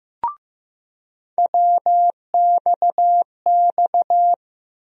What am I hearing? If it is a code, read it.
Morse code, heard as a keyed tone, about 15 wpm, 700 Hz. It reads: WXX
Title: USA Call Sign Postfixes - 3 Letters 15wpm